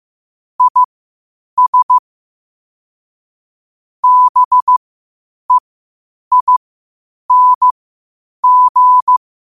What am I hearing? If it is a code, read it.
Morse code, heard as a keyed tone, about 15 wpm, spaced as for 8 wpm, 1000 Hz. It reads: IS BEING
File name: qotdfiles / 8wpm